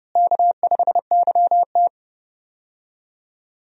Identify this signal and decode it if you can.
Morse code, heard as a keyed tone, about 30 wpm, 700 Hz. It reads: K5YT
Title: USA Call Signs - 4 Characters - Repeated 2x - 30wpm